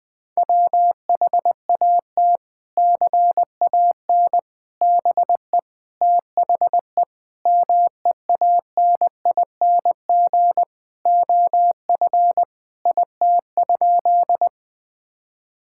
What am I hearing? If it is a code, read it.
Morse code, heard as a keyed tone, about 20 wpm, 700 Hz. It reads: WHAT CAN BE THE MEANING OF IT?
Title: Sentences from Top 1000 Words - Review Part 6 - 20wpm